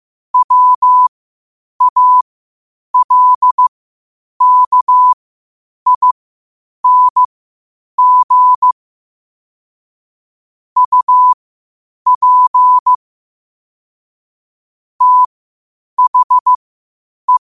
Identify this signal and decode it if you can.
Morse code, heard as a keyed tone, about 15 wpm, spaced as for 8 wpm, 1000 Hz. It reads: WALKING UP THE
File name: qotdfiles / 8wpm